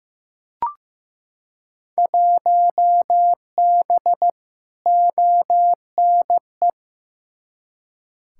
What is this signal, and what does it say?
Morse code, heard as a keyed tone, about 15 wpm, 700 Hz. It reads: 1B ONE
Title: Field Day Exchanges - 15wpm